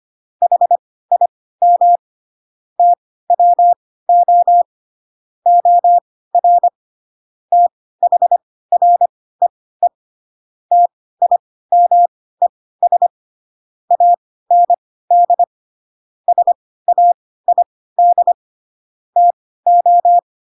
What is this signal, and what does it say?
Morse code, heard as a keyed tone, about 25 wpm, spaced as for 16 wpm, 700 Hz. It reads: HIM TWO OR THREE TIMES AND SAID TO
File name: Pinocchio / Pinocchio ch3 10 25 ik1hge